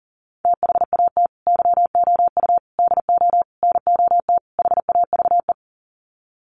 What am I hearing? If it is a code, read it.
Morse code, heard as a keyed tone, about 40 wpm, 700 Hz. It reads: THAT YOU DO NOT HAVE
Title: Sets of 5 Words Encore 40wpm